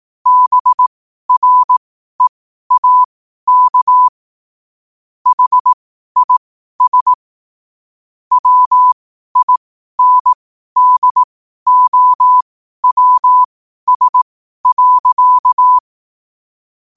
Morse code, heard as a keyed tone, about 18 wpm, spaced as for 12 wpm, 1000 Hz. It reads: BREAK HIS WINDOWS.